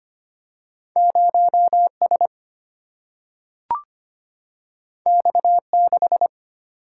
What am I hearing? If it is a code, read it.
Morse code, heard as a keyed tone, about 25 wpm, 700 Hz. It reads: 0 S X 6
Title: Single-Letter Single-Number 25wpm